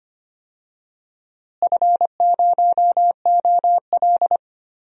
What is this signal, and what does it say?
Morse code, heard as a keyed tone, about 25 wpm, 700 Hz. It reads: F0OL
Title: DXCC-100 Call Signs - 4 Characters - Repeated 3x - 25wpm